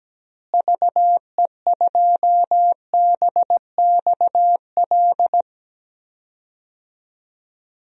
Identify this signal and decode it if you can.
Morse code, heard as a keyed tone, about 17 wpm, 700 Hz. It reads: VE2BXL
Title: DXCC-100 Call Signs - 6 Characters - Repeated 2x - 17wpm